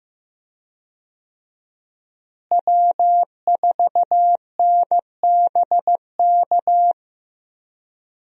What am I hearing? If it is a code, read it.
Morse code, heard as a keyed tone, about 16 wpm, 700 Hz. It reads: W4NBK
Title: USA Call Signs - 5 Characters - Repeated 2x - 15wpm